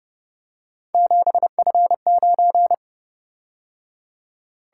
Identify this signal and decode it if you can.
Morse code, heard as a keyed tone, about 29 wpm, 700 Hz. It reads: 7F9
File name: Speed-Racing - Three Letter-Number Combinations - 25wpm